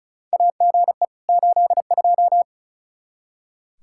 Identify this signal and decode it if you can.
Morse code, heard as a keyed tone, about 36 wpm, 700 Hz. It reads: AGE 82
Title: K1USN SST QSO Practice 35wpm